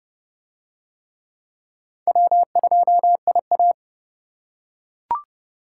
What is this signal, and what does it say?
Morse code, heard as a keyed tone, about 28 wpm, 700 Hz. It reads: W2IA